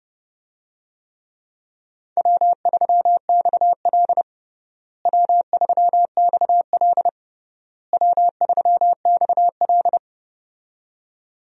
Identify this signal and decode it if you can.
Morse code, heard as a keyed tone, about 30 wpm, 700 Hz. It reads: W3XL W3XL W3XL
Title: USA Call Signs - 4 Characters - Repeated 3x - 30wpm